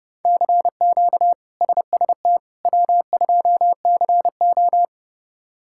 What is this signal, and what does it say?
Morse code, heard as a keyed tone, about 30 wpm, 700 Hz. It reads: CQ SST W2CO
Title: K1USN SST QSO Practice - 30wpm